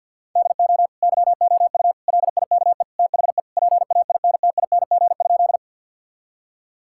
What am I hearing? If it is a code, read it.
Morse code, heard as a keyed tone, about 52 wpm, 700 Hz. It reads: DO YOU LIKE THE PAINTING?
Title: Sentences from Top 1000 Words - Review Part 6 - 50wpm